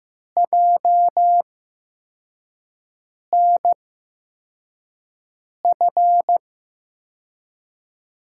Morse code, heard as a keyed tone, about 15 wpm, spaced as for 4 wpm, 700 Hz. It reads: JNF
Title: USA Call Signs - 5 Characters - 8x Character Spacing - 15wpm